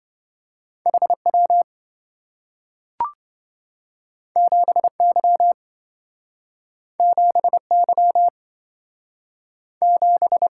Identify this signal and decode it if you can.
Morse code, heard as a keyed tone, about 27 wpm, 700 Hz. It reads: HW 7Y 7Y 7